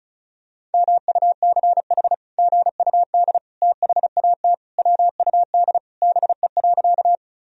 Morse code, heard as a keyed tone, about 34 wpm, 700 Hz. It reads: MUCH GUD THAT WUD BE.
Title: Sentences from Top 200 Words and Abbreviations 35wpm